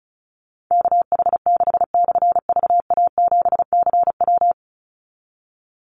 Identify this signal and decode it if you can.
Morse code, heard as a keyed tone, about 36 wpm, 700 Hz. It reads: KH6/VA7CW
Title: Call Signs - 35wpm